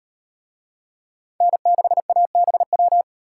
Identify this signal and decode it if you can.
Morse code, heard as a keyed tone, about 38 wpm, 700 Hz. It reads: N6ABW